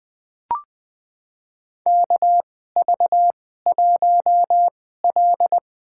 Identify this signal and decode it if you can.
Morse code, heard as a keyed tone, about 20 wpm, spaced as for 15 wpm, 700 Hz. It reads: KV1L
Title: USA Call Signs - 2x Character Spacing - 20wpm